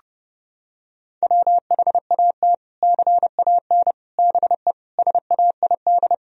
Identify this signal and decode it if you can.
Morse code, heard as a keyed tone, about 30 wpm, 700 Hz. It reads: WHAT CAN BE SAID
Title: Sentences from Top 1000 Words - Review Part 4 - 30wpm